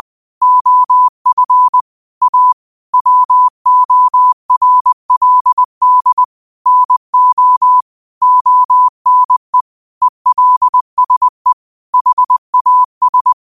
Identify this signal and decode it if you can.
Morse code, heard as a keyed tone, about 20 wpm, 1000 Hz. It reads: OF A WORLD NO ONE ELSE HAS